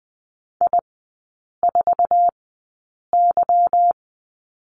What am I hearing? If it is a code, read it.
Morse code, heard as a keyed tone, about 20 wpm, spaced as for 8 wpm, 700 Hz. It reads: I4Y
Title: 159 Lesson 26e - Letter g - Three Characters 20wpm